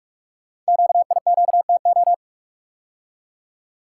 Morse code, heard as a keyed tone, about 44 wpm, 700 Hz. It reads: QIQTO
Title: Code Groups - 5 Letters - 45wpm